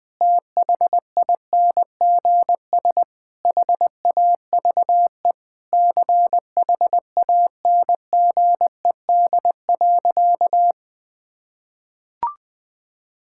Morse code, heard as a keyed tone, about 20 wpm, 700 Hz. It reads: THINGS HAVE CHANGED.
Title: Sentences with 3-Word Phrases - Spoken Beginning - 20wpm